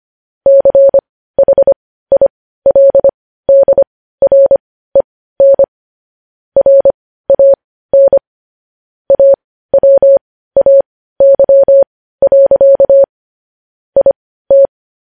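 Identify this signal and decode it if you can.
Morse code, heard as a keyed tone, about 25 wpm, spaced as for 15 wpm, 550 Hz. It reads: CHILDREN RAN AWAY. IT
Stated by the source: Morse news headlines sent at 15 words per minute.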